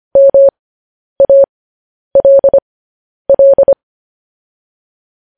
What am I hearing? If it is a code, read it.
Morse code, heard as a keyed tone, about 25 wpm, spaced as for 10 wpm, 550 Hz. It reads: MALL
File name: podcast / News headlines-1x-25x10